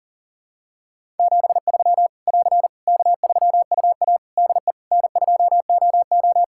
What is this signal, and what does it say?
Morse code, heard as a keyed tone, about 40 wpm, 700 Hz. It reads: 73 <AR> K3UA DE N2OO
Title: K1USN SST QSO Practice 40wpm